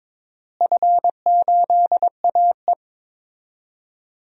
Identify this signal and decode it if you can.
Morse code, heard as a keyed tone, about 22 wpm, 700 Hz. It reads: F8AE
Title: DXCC-100 Call Signs - 4 Characters - Repeated 3x - 22wpm